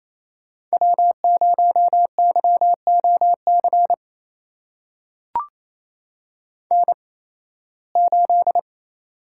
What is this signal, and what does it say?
Morse code, heard as a keyed tone, about 28 wpm, spaced as for 18 wpm, 700 Hz. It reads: W0YOC N8
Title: USA Call Signs - 5 Characters - 8x Character Spacing - 28wpm